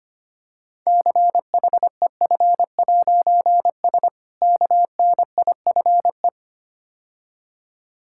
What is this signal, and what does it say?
Morse code, heard as a keyed tone, about 26 wpm, 700 Hz. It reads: CHEF'S KNIFE
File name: practice / 25wpm / Difficult Words - Speed-Racing with Hints - 25wpm